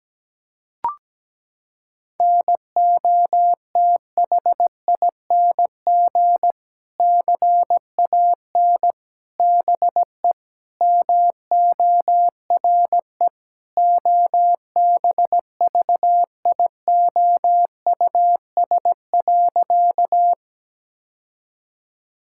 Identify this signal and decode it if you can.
Morse code, heard as a keyed tone, about 17 wpm, 700 Hz. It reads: NOTHING CAN BE MORE OBVIOUS.
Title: Sentences from Top 1600 Words 17wpm